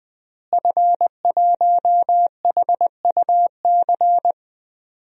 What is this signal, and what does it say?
Morse code, heard as a keyed tone, about 20 wpm, 700 Hz. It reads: F1HUC
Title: DXCC-100 Call Signs - 5 Characters - Repeated 3x - 20wpm